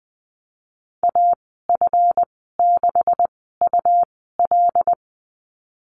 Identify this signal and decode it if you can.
Morse code, heard as a keyed tone, about 19 wpm, spaced as for 14 wpm, 700 Hz. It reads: AF6UL
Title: 179 Lesson 29g - Slash - Call Signs 20wpm